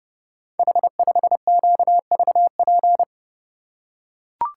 Morse code, heard as a keyed tone, about 30 wpm, 700 Hz. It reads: H5QVP
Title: Code Groups - 5 Letter:Numbers - 30wpm